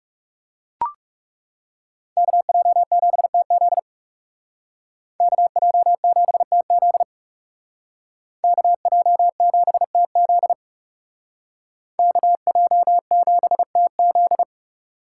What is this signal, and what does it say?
Morse code, heard as a keyed tone, about 36 wpm, 700 Hz. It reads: KJ7TZ KJ7TZ KJ7TZ KJ7TZ
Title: Speed Racing - USA 2x2 Call Signs 30wpm